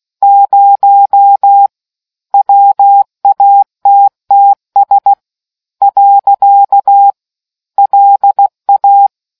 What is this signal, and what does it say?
Morse code, heard as a keyed tone, about 16 wpm, 800 Hz. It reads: 0 WATTS . LA